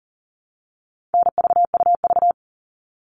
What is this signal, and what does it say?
Morse code, heard as a keyed tone, about 40 wpm, 700 Hz. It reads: NVUV
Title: All Cut Numbers - 4-Digit 40wpm